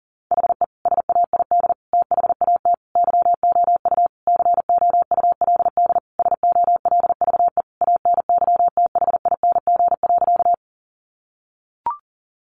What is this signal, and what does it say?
Morse code, heard as a keyed tone, about 40 wpm, 700 Hz. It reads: HE SAID THAT YOU COULD SOLVE ANYTHING.
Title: Sentences from Top 1000 Words 40wpm